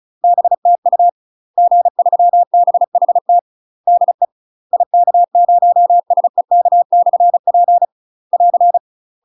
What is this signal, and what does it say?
Morse code, heard as a keyed tone, about 36 wpm, 700 Hz. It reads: BTU G3BHT DE IK0SEK/P <AR>
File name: UK MASTER QSOs TRAINING / 35WPM / UK MASTER QSOs 0018